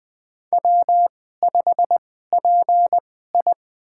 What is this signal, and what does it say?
Morse code, heard as a keyed tone, about 20 wpm, spaced as for 15 wpm, 700 Hz. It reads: W5PI
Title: USA Call Signs - 4 Characters - 2x Character Spacing - 20wpm